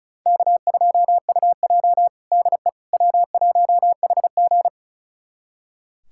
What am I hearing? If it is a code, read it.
Morse code, heard as a keyed tone, about 36 wpm, 700 Hz. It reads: K2UJ DE W1HG
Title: USA Callsign DE Callsign - 4 Characters - 35wpm